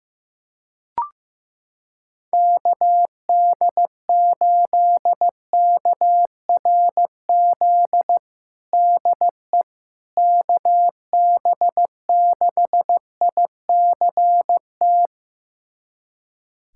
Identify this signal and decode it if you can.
Morse code, heard as a keyed tone, about 15 wpm, 700 Hz. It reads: KD8KRZ DE KB6ICT
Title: USA Callsign DE Callsign 15wpm